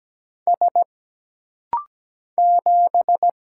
Morse code, heard as a keyed tone, about 17 wpm, 700 Hz. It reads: S 7
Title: Single Letter-Number Alt - Rapid-Fire 17wpm